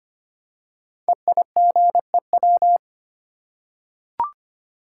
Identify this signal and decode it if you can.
Morse code, heard as a keyed tone, about 25 wpm, 700 Hz. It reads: EIGEW
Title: Code Groups - 5 Letters - 25wpm